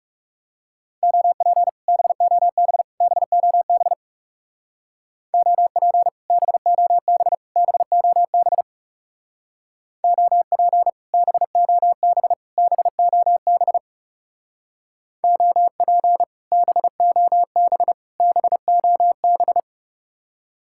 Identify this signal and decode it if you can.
Morse code, heard as a keyed tone, about 38 wpm, 700 Hz. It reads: OP BOB BOB OP BOB BOB OP BOB BOB OP BOB BOB